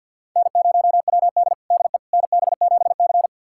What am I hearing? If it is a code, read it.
Morse code, heard as a keyed tone, about 45 wpm, 700 Hz. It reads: N0WD DE NB7C
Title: USA Callsign DE Callsign - 4 Characters - 50wpm